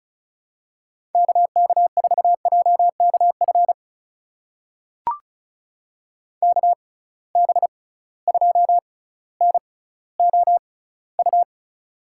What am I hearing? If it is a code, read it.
Morse code, heard as a keyed tone, about 36 wpm, spaced as for 18 wpm, 700 Hz. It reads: KK4JKF KB2NOU
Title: USA Call Signs - 6 Characters - 6x Character Spacing - 35wpm